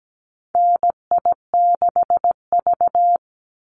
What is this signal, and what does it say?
Morse code, heard as a keyed tone, about 17 wpm, 700 Hz. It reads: NI6V
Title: Call Signs - 17wpm